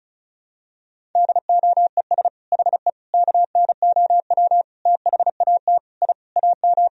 Morse code, heard as a keyed tone, about 34 wpm, 700 Hz. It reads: DOES HE KNOW THAT I AM
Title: Sentences from Top 1000 Words - Review Part 4 - 35wpm